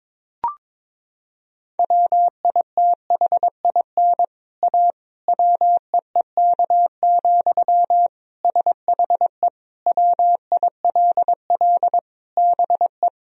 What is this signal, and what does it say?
Morse code, heard as a keyed tone, about 22 wpm, 700 Hz. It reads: WITHIN A WEEK, SHE WILL BE
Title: Sentences from Top 1000 Words - Review Part 4 - 22wpm